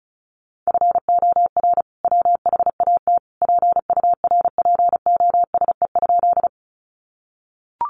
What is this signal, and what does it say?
Morse code, heard as a keyed tone, about 34 wpm, 700 Hz. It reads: FOR WHAT PURPOSE?
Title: Sentences from Top 800 Words 35wpm